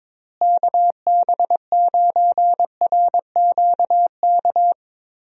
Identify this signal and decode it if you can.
Morse code, heard as a keyed tone, about 21 wpm, 700 Hz. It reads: KB9RQK
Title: USA Call Signs - 6 Characters - Repeated 2x - 22wpm